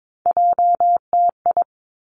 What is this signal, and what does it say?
Morse code, heard as a keyed tone, about 22 wpm, 700 Hz. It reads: JTI
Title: Three Challenging Letter-Number Combinations 22wpm